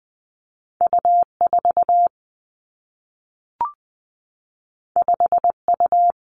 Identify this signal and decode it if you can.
Morse code, heard as a keyed tone, about 20 wpm, 700 Hz. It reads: U 4 5 U
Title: Two Challenging Letter-Number Combinations 20wpm